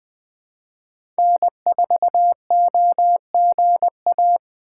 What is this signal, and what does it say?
Morse code, heard as a keyed tone, about 20 wpm, 700 Hz. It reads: N4OGA
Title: USA Call Signs - 5 Characters - 3x Character Spacing - 20wpm